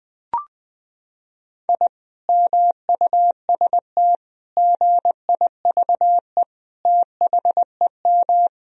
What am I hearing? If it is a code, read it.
Morse code, heard as a keyed tone, about 21 wpm, 700 Hz. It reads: I MUST GIVE THEM
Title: Sentences from Top 1000 Words - Review Part 2 - 20wpm